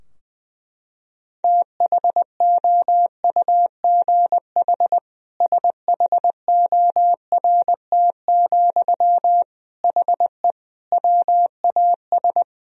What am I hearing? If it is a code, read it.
Morse code, heard as a keyed tone, about 20 wpm, 700 Hz. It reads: THOUGH SHORT, HE WAS
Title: Sentences from Top 1000 Words - Review Part 3 - 20wpm